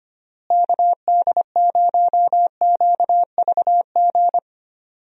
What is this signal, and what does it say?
Morse code, heard as a keyed tone, about 25 wpm, 700 Hz. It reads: KD0QVG
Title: USA Call Signs - 6 Characters - Repeated 3x - 25wpm